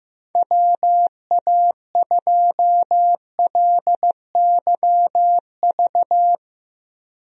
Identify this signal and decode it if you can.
Morse code, heard as a keyed tone, about 15 wpm, 700 Hz. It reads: WA2LYV
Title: USA Call Signs - 6 Characters - 15wpm